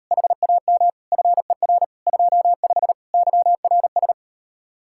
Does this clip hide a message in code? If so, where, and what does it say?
Yes: Morse code from 0.1 to 4.1 s, heard as a keyed tone, about 38 wpm, 700 Hz. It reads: HAM FER 25 YRS